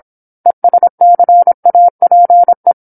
Morse code, heard as a keyed tone, about 27 wpm, 700 Hz. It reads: ESCAPE
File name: 08-Chapter-08 26 WPM